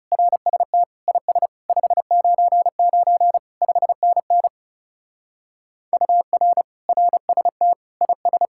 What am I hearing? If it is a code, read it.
Morse code, heard as a keyed tone, about 36 wpm, 700 Hz. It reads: RST IS 599 5NN UR RST IS
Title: Speed-Racing Common QSO Phrases Encore - 30wpm